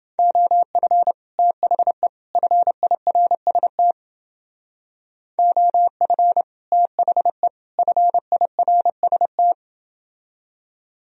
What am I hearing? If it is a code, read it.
Morse code, heard as a keyed tone, about 30 wpm, 700 Hz. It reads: OF THE FIRST OF THE FIRST